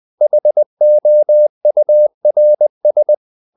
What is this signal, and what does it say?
Morse code, heard as a keyed tone, about 20 wpm, 600 Hz. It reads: HOURS
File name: the old man and the sea / the old man and the sea part 1 20wpm